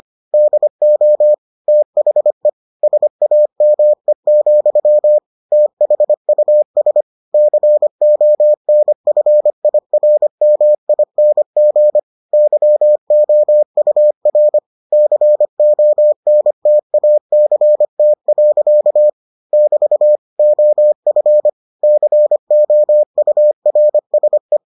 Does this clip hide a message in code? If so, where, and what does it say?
Yes: Morse code from 0.3 to 24.6 s, heard as a keyed tone, about 25 wpm, 600 Hz. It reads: DO THE SAME, THUS CONFIRMING YOUR CONTACT. <BT> OF COURSE